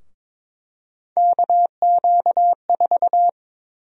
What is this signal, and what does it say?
Morse code, heard as a keyed tone, about 22 wpm, 700 Hz. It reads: KQ4
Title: USA Call Sign Prefixes - 22wpm